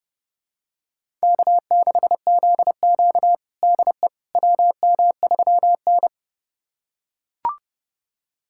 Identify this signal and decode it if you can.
Morse code, heard as a keyed tone, about 30 wpm, 700 Hz. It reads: K6ZQ DE WM3N
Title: USA Callsign DE Callsign - 4 Characters - 30wpm